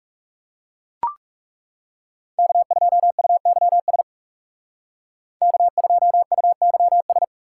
Morse code, heard as a keyed tone, about 39 wpm, 700 Hz. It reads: K2UYS K2UYS